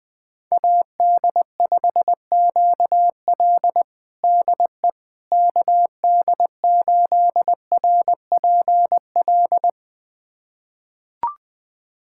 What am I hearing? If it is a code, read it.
Morse code, heard as a keyed tone, about 20 wpm, 700 Hz. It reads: AD5QL DE KD8RPL
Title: USA Callsign DE Callsign 20wpm